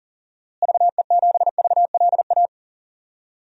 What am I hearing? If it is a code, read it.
Morse code, heard as a keyed tone, about 40 wpm, 700 Hz. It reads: VE7VLA
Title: DXCC-100 Call Signs - 6 Characters - Repeated 2x - 40wpm